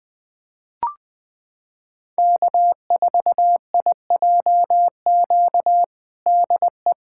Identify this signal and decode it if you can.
Morse code, heard as a keyed tone, about 21 wpm, 700 Hz. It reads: K4IJQ DE
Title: USA Callsign DE Callsign 20wpm